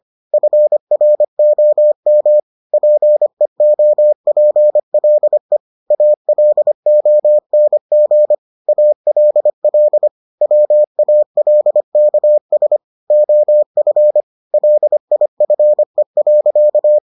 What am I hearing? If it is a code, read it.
Morse code, heard as a keyed tone, about 25 wpm, 600 Hz. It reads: FROM PEOPLE ALONG ALL WALKS OF LIFE.